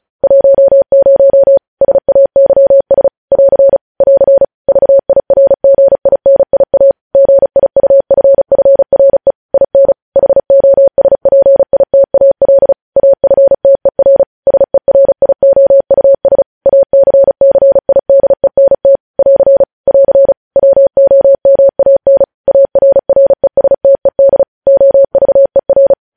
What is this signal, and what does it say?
Morse code, heard as a keyed tone, about 36 wpm, 550 Hz. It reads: 10 SAYS <AR> <AR> VIRGINIA GIUFFRE IN HOSPITAL AFTER SERIOUS ACCIDENT <AR> <AR> WOMAN ARRESTED OVER
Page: News Headlines in Morse Code at 35 WPM